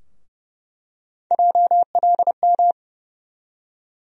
Morse code, heard as a keyed tone, about 30 wpm, 700 Hz. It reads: JLM
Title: USA Call Sign Postfixes - 3 Letters 30wpm